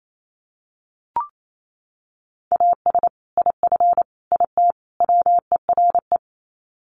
Morse code, heard as a keyed tone, about 28 wpm, 700 Hz. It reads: AS IF IT WERE
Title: Sets of 4 Words 28wpm